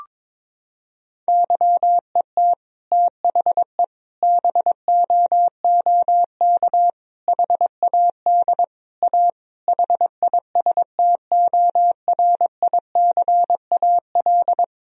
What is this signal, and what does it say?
Morse code, heard as a keyed tone, about 22 wpm, 700 Hz. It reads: YET THE BOOK HAD A HISTORICAL
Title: Sentences from Top 1000 Words - Review Part 4 - 22wpm